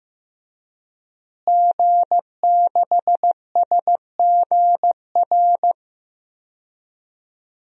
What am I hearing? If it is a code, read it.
Morse code, heard as a keyed tone, about 15 wpm, 700 Hz. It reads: G6SGR
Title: DXCC-100 Call Signs - 5 Characters - Repeated 3x - 15wpm